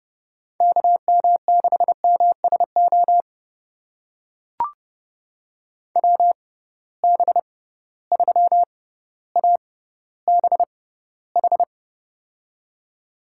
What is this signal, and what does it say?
Morse code, heard as a keyed tone, about 30 wpm, spaced as for 15 wpm, 700 Hz. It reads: KM6MSO WB3ABH
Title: USA Call Signs - 6 Characters - 6x Character Spacing - 30wpm